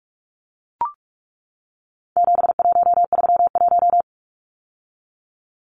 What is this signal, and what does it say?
Morse code, heard as a keyed tone, about 44 wpm, 700 Hz. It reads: 7131
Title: Numbers - 4-Digit 45wpm